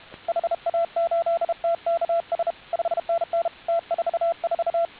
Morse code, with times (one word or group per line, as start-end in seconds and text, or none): 0.3-2.5 HA8TKS
2.7-3.5 5NN
3.7-4.8 T44